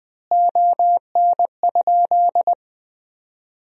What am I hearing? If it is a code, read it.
Morse code, heard as a keyed tone, about 21 wpm, 700 Hz. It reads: ON?
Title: Sentences from Top 1000 Words - Review Part 3 - 20wpm